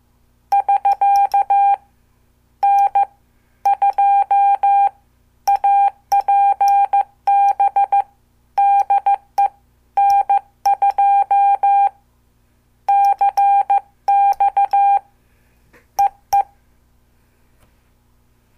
Morse code, heard as a keyed tone, about 14 wpm, spaced as for 11 wpm, 800 Hz. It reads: <SK> N2APB DE N2CX EE
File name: morse / qso